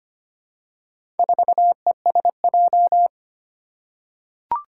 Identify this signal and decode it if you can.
Morse code, heard as a keyed tone, about 25 wpm, 700 Hz. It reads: 4ESJ
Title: Four Challenging Letter-Number Combinations 25wpm